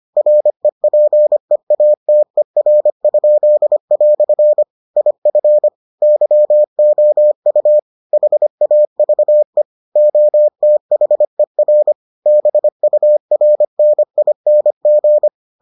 Morse code, heard as a keyed tone, about 25 wpm, 600 Hz. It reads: REPEATER?" IF YOU HAVE OTHER BURNING